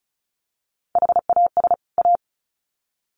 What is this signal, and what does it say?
Morse code, heard as a keyed tone, about 34 wpm, 700 Hz. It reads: HAS A
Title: Sets of 2 Words 35wpm